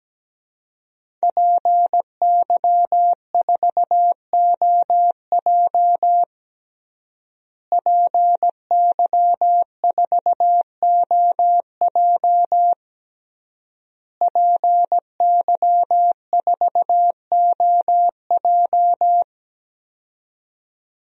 Morse code, heard as a keyed tone, about 17 wpm, 700 Hz. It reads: PY4OJ PY4OJ PY4OJ
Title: DXCC-100 Call Signs - 5 Characters - Repeated 3x - 17wpm